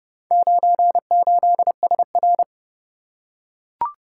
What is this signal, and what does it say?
Morse code, heard as a keyed tone, about 30 wpm, 700 Hz. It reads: 98SR